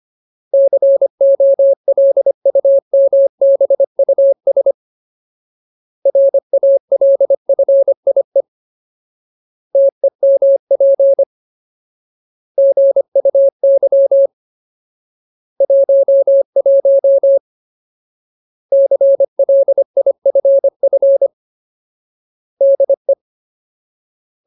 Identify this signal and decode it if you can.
Morse code, heard as a keyed tone, about 25 wpm, 550 Hz. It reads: COLUMBUS RALFIE TEMP GUY 11 CLIFF DE
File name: Ultimate Word List / 25WPM Ultimate